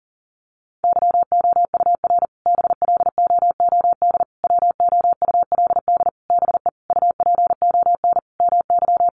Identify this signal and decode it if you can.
Morse code, heard as a keyed tone, about 40 wpm, 700 Hz. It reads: YOUR BLOOD WOULD BE UPON MY
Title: Sentences from Top 1100 Words 40wpm